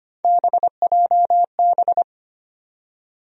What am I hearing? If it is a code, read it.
Morse code, heard as a keyed tone, about 25 wpm, 700 Hz. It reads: BJB
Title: USA Call Sign Postfixes - 3 Letters 25wpm